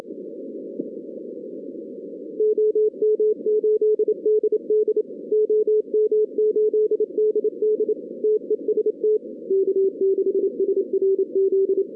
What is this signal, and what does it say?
Morse code, heard as a keyed tone, about 26 wpm, 440 Hz. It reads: OM8DD OM8DD TEST K6SRZ
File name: 2016 arrl dx cw / 2016-ARRL-DX-CW-K5TR- - 0-41